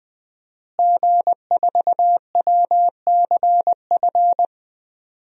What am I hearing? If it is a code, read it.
Morse code, heard as a keyed tone, about 20 wpm, 700 Hz. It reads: G4WCF
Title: DXCC-100 Call Signs - 5 Characters - Repeated 3x - 20wpm